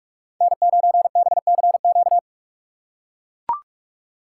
Morse code, heard as a keyed tone, about 41 wpm, 700 Hz. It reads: N9BCQ
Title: USA Call Signs - 5 Characters - 4x Character Spacing - 45wpm